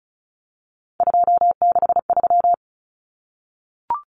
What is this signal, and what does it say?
Morse code, heard as a keyed tone, about 34 wpm, 700 Hz. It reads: 263
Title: Numbers - 3-Digit 35wpm